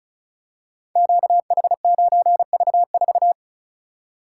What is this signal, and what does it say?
Morse code, heard as a keyed tone, about 36 wpm, 700 Hz. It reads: QH9V4
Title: Code Groups - 5 Letter:Numbers - 35wpm